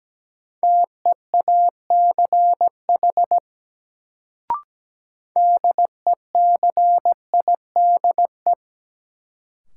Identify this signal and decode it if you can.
Morse code, heard as a keyed tone, about 17 wpm, 700 Hz. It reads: TEACH DECIDE
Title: Sentences from Top 500 Words - New Word Drill - 17wpm